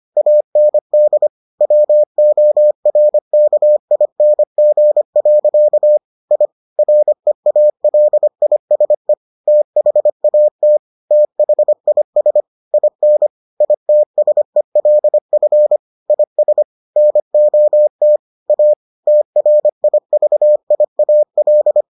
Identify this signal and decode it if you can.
Morse code, heard as a keyed tone, about 25 wpm, 600 Hz. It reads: AND WORKING. I REALISE THAT THIS IN ITSELF IS NOT A TRIVIAL